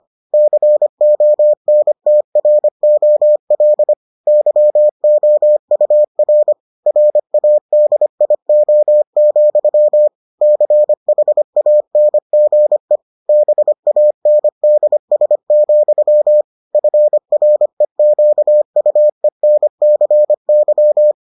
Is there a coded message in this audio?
Yes: Morse code from 0.3 to 21.1 s, heard as a keyed tone, about 25 wpm, 600 Hz. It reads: CONTROL YOUR RADIO, CHANGE BANDS, FREQUENCY